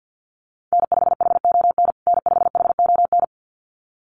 Morse code, heard as a keyed tone, about 48 wpm, 700 Hz. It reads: N5HON N5HON
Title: USA Call Signs Repeated 2x 50wpm